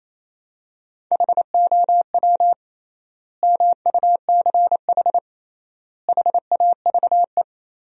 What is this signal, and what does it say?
Morse code, heard as a keyed tone, about 28 wpm, 700 Hz. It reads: HOW MUCH HAVE
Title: 3x Word-Spacing - Sentences from Top 300 Words - 28wpm